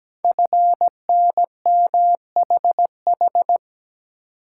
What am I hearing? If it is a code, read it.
Morse code, heard as a keyed tone, about 17 wpm, 700 Hz. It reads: FNMHH
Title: Code Groups - 5 Letters - 17wpm